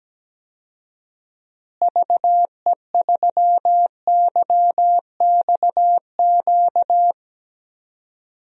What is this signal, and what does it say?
Morse code, heard as a keyed tone, about 17 wpm, 700 Hz. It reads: VE3YXQ
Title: DXCC-100 Call Signs - 6 Characters - Repeated 3x - 17wpm